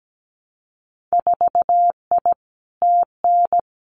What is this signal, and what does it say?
Morse code, heard as a keyed tone, about 17 wpm, 700 Hz. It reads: 4I TN
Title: Winter Field Day Exchanges 17wpm